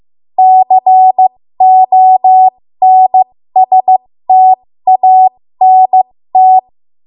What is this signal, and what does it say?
Morse code, heard as a keyed tone, about 15 wpm, spaced as for 13 wpm, 750 Hz. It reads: CONSTANT
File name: Morse / 13WPM